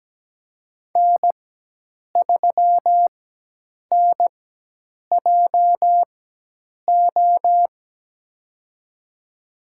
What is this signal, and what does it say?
Morse code, heard as a keyed tone, about 17 wpm, spaced as for 8 wpm, 700 Hz. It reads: N3NJO
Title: USA Call Signs - 5 Characters - 4x Character Spacing - 17wpm